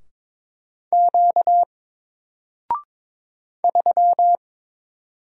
Morse code, heard as a keyed tone, about 22 wpm, 700 Hz. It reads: Q 3
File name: Single-Letter-Number-22wpm